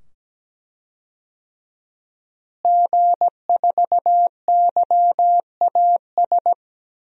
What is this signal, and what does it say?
Morse code, heard as a keyed tone, about 17 wpm, 700 Hz. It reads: G4YAS
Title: DXCC-100 Call Signs - 5 Characters - Repeated 2x - 17wpm